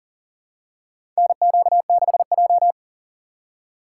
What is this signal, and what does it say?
Morse code, heard as a keyed tone, about 40 wpm, 700 Hz. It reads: NQ6J
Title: USA Call Signs - 4 Characters - 2x Character Spacing - 40wpm